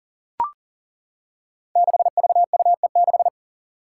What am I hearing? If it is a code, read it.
Morse code, heard as a keyed tone, about 42 wpm, 700 Hz. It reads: 6VUE6
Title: Five Challenging Letter-Number Combinations 40wpm